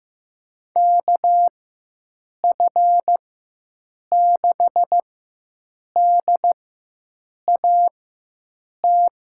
Morse code, heard as a keyed tone, about 15 wpm, spaced as for 7 wpm, 700 Hz. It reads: KF6DAT
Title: USA Call Signs - 4x Character Spacing - 15wpm